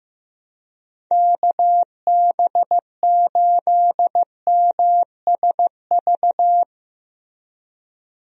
Morse code, heard as a keyed tone, about 15 wpm, 700 Hz. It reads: KB8MSV
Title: USA Call Signs - 6 Characters - Repeated 2x - 15wpm